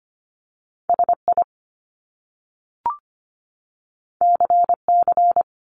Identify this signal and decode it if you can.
Morse code, heard as a keyed tone, about 24 wpm, 700 Hz. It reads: SI CC
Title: Top Two Letter Combinations in Words 25wpm